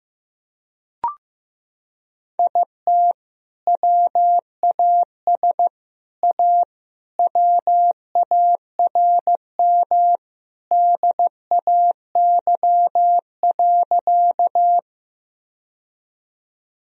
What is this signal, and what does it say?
Morse code, heard as a keyed tone, about 15 wpm, 700 Hz. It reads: IT WAS A WARM DAY.
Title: Sentences from Top 1000 Words - Review Part 2 - 15wpm